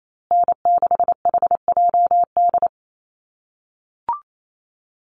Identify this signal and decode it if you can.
Morse code, heard as a keyed tone, about 28 wpm, 700 Hz. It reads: N6HJD
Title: USA Call Signs Repeated 3x 28wpm